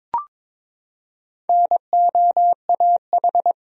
Morse code, heard as a keyed tone, about 22 wpm, 700 Hz. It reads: NOAH
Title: USA Common Names - Max 4 Letters - 22wpm